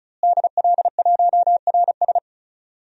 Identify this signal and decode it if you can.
Morse code, heard as a keyed tone, about 34 wpm, 700 Hz. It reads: DL1RS
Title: DXCC-100 Call Signs - 5 Characters - Repeated 3x - 35wpm